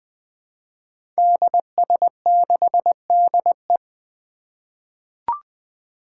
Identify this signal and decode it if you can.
Morse code, heard as a keyed tone, about 20 wpm, 700 Hz. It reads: DS6DE
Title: Five Challenging Letter-Number Combinations 20wpm